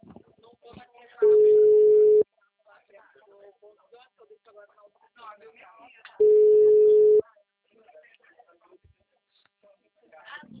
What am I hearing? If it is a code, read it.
Ringback tone (Europe, 425 Hz), with speech.